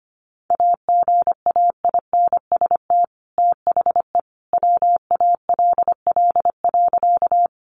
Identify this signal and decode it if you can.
Morse code, heard as a keyed tone, about 25 wpm, 700 Hz. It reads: AGAINST THE WALL.
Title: Sentences from Top 1100 Words 25wpm